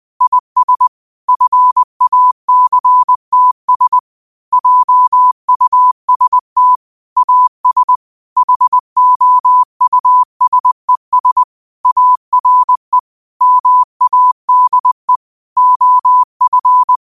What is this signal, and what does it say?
Morse code, heard as a keyed tone, about 20 wpm, 1000 Hz. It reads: IS FACTS JUST AS HOUSES ARE MADE OF
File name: qotdfiles / 20wpm